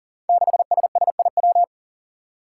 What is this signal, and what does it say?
Morse code, heard as a keyed tone, about 38 wpm, 690 Hz. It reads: 6SSIW
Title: Code Groups - 5 Letter:Numbers - 40wpm